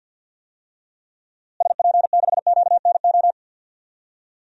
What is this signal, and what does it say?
Morse code, heard as a keyed tone, about 52 wpm, 700 Hz. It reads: SP6QNO
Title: DXCC-100 Call Signs - 6 Characters - Repeated 3x - 50wpm